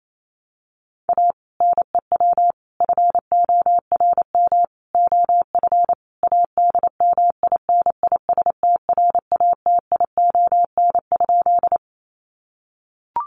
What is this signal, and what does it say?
Morse code, heard as a keyed tone, about 28 wpm, 700 Hz. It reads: A NEW FORM OF ADMINISTRATION?
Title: Sentences from Top 1300 Words 28wpm